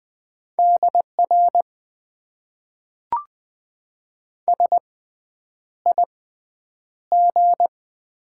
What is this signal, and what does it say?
Morse code, heard as a keyed tone, about 20 wpm, spaced as for 9 wpm, 700 Hz. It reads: DR SIG
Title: Top 90 QSO Elements - 6x Character Spacing - 20wpm